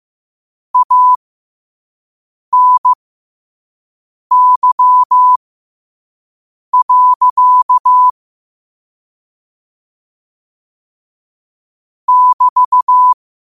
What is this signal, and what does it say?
Morse code, heard as a keyed tone, about 15 wpm, spaced as for 5 wpm, 1000 Hz. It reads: ANY. <BT>
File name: qotdfiles / 5wpm